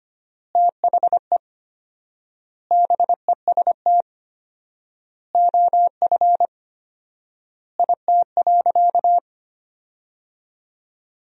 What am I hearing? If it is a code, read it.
Morse code, heard as a keyed tone, about 25 wpm, 700 Hz. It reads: THE BEST OF IT.
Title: 4x Word-Spacing - Sentences from Top 400 Words - 25wpm